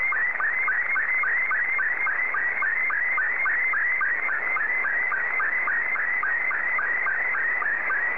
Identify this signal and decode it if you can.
SSTV (slow-scan television), Scottie 2.